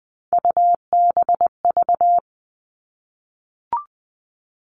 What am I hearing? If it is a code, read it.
Morse code, heard as a keyed tone, about 20 wpm, 700 Hz. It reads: UBV
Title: All Cut Numbers - 3-Digit 20wpm